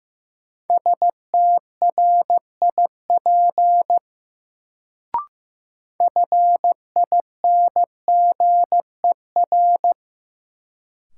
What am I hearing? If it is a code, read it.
Morse code, heard as a keyed tone, about 15 wpm, 700 Hz. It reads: STRIP FINGER
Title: Sentences from Top 1600 Words - New Word Drill - 15wpm